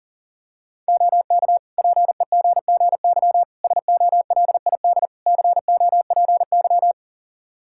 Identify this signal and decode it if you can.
Morse code, heard as a keyed tone, about 40 wpm, 700 Hz. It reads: OK PEGGY SOLID COPY
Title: Practice QSO - Exchange 3 - More Info - 40wpm